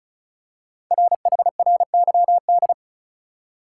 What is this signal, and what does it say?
Morse code, heard as a keyed tone, about 36 wpm, 690 Hz. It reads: RHRYD
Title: Code Groups - 5 Letters - 35wpm